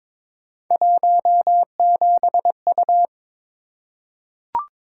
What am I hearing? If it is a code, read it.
Morse code, heard as a keyed tone, about 22 wpm, 700 Hz. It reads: 17U
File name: practice / 22wpm / Three Letter-Number Combinations - 22wpm